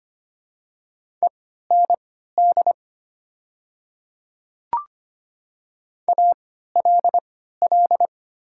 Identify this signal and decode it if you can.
Morse code, heard as a keyed tone, about 25 wpm, spaced as for 14 wpm, 700 Hz. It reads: END ALL
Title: 3x Character Spacing - Top 200 Words - 25wpm